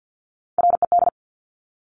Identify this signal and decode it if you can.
Morse code, heard as a keyed tone, about 50 wpm, 700 Hz. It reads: RED
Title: Top Three Letter Combinations in Words 50wpm